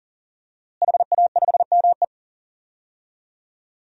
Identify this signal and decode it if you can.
Morse code, heard as a keyed tone, about 40 wpm, 700 Hz. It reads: HA5ME
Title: DXCC-100 Call Signs - 5 Characters - Repeated 3x - 40wpm